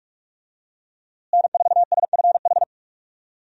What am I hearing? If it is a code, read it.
Morse code, heard as a keyed tone, about 46 wpm, 700 Hz. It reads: N4SFH
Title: USA Call Signs - 5 Characters - Repeated 2x - 45wpm